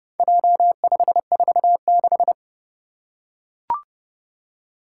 Morse code, heard as a keyed tone, about 30 wpm, 700 Hz. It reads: J546
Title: Four Challenging Letter-Number Combinations 30wpm